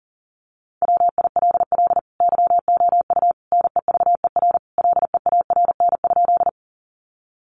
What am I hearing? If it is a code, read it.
Morse code, heard as a keyed tone, about 38 wpm, 700 Hz. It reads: WILL YOU NEVER LEARN?
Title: Sentences from Top 300 Words 40wpm